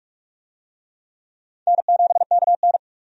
Morse code, heard as a keyed tone, about 47 wpm, 700 Hz. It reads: N7KN
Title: USA Call Signs - 4 Characters - 45wpm